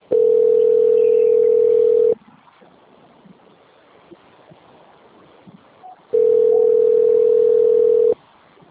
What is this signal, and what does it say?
Ringback tone (North America), 440 + 480 Hz.